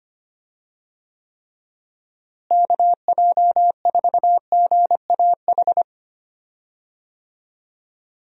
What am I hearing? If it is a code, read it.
Morse code, heard as a keyed tone, about 25 wpm, 700 Hz. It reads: KJ4GAH
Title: USA Call Signs - 6 Characters - Repeated 3x - 25wpm